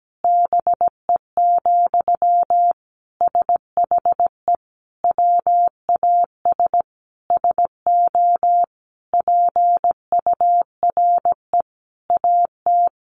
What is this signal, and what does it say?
Morse code, heard as a keyed tone, about 17 wpm, 700 Hz. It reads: BE, SHE WAS SO PURE AT
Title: Sentences from Top 1500 Words 17wpm